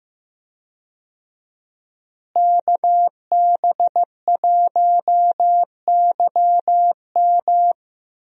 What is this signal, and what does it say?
Morse code, heard as a keyed tone, about 15 wpm, 700 Hz. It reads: KB1YM
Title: USA Call Signs - 5 Characters - Repeated 2x - 15wpm